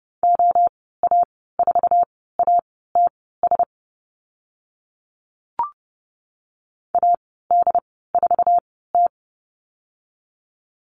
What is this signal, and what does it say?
Morse code, heard as a keyed tone, about 30 wpm, spaced as for 17 wpm, 700 Hz. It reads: OA4ATS AD4T
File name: Lesson 15g - Number 5 - Call Signs 30wpm